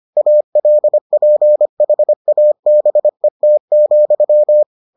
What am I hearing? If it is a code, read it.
Morse code, heard as a keyed tone, about 25 wpm, 600 Hz. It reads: ALPHABET,